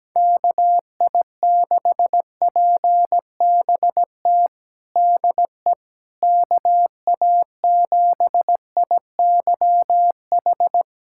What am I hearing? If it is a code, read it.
Morse code, heard as a keyed tone, about 17 wpm, 700 Hz. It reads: KI6PBT DE KA7IYH
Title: USA Callsign DE Callsign - 6 Characters - 17wpm